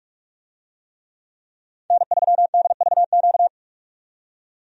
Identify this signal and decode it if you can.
Morse code, heard as a keyed tone, about 43 wpm, 700 Hz. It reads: N3DVQ
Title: USA Call Signs - 5 Characters - 45wpm